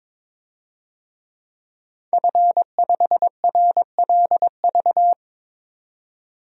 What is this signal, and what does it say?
Morse code, heard as a keyed tone, about 22 wpm, 700 Hz. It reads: F5RLV
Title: DXCC-100 Call Signs - 5 Characters - 22wpm